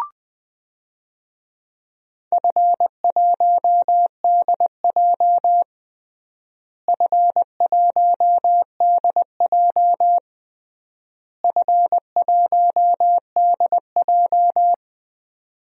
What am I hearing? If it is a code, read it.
Morse code, heard as a keyed tone, about 21 wpm, 700 Hz. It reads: F1DJ F1DJ F1DJ
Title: DXCC-100 Call Signs - 4 Characters - Repeated 3x - 20wpm